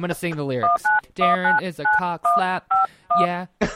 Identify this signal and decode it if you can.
DTMF, heard as telephone touch tones, tones of about 160 ms, with speech. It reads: 19199151